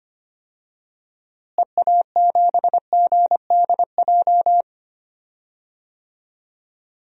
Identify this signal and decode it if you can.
Morse code, heard as a keyed tone, about 25 wpm, 700 Hz. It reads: EA7GDJ
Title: DXCC-100 Call Signs - 6 Characters - Repeated 3x - 25wpm